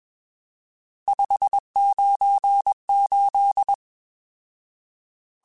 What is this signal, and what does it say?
Morse code, heard as a keyed tone, about 21 wpm, 800 Hz. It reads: 598
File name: RSTs / RST-mp3s / Various RST @20